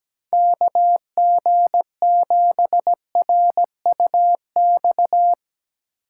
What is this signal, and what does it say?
Morse code, heard as a keyed tone, about 17 wpm, 700 Hz. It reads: KG7RUX
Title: USA Call Signs - 6 Characters - 3x Character Spacing - 17wpm